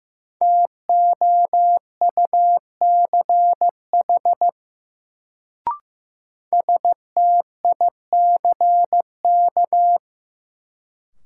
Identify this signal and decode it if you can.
Morse code, heard as a keyed tone, about 15 wpm, 700 Hz. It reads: TOUCH STICK
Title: Sentences from Top 1200 Words - New Word Drill - 15wpm